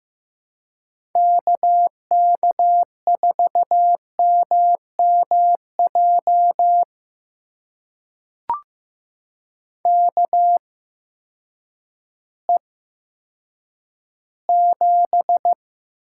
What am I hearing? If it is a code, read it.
Morse code, heard as a keyed tone, about 15 wpm, spaced as for 8 wpm, 700 Hz. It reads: KK4MMJ KE7
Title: USA Call Signs - 6 Characters - 8x Character Spacing - 15wpm